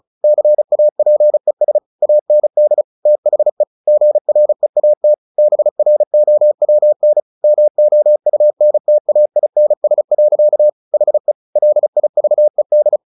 Morse code, heard as a keyed tone, about 34 wpm, 600 Hz. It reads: CAPES AND THE GREAT BROWN MOUNTAINS. HE LIVED
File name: the old man and the sea / the old man and the sea 1 35wpm